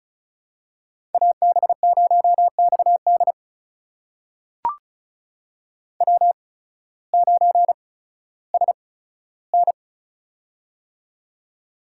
Morse code, heard as a keyed tone, about 36 wpm, spaced as for 16 wpm, 700 Hz. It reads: AB0XD W9SN
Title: USA Call Signs - 8x Character Spacing - 35wpm